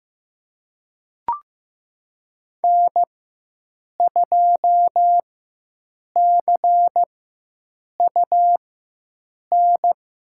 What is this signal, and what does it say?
Morse code, heard as a keyed tone, about 15 wpm, spaced as for 7 wpm, 700 Hz. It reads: N2CUN
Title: USA Call Signs - 5 Characters - 4x Character Spacing - 15wpm